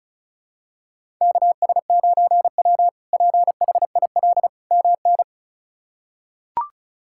Morse code, heard as a keyed tone, about 34 wpm, 700 Hz. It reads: KS9W PHIL MN
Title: K1USN SST QSO Practice - 35wpm